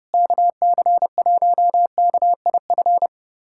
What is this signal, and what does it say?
Morse code, heard as a keyed tone, about 30 wpm, 700 Hz. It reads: KC1KIF
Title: USA Callsign DE Callsign - 6 Characters - 30wpm